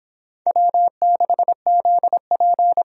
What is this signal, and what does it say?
Morse code, heard as a keyed tone, about 26 wpm, 700 Hz. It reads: W6ZP